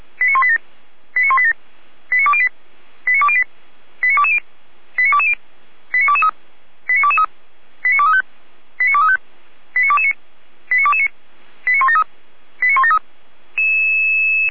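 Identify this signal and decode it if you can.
Five-tone selective call (ZVEI), heard as quick runs of five tones, tones of about 70 ms. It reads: 89187 89187 89208 89208 89220 89220 89223 89223 89235 89235 89229 89229 89173 89173
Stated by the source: ZVEI five-tone call received by SDR, archived by the Orlamünde fire brigade (feuerwehr-orlamuende.org) server.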